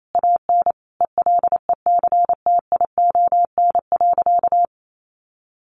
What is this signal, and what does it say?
Morse code, heard as a keyed tone, about 28 wpm, 700 Hz. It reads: AN ELECTION.
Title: Sentences from Top 1500 Words 28wpm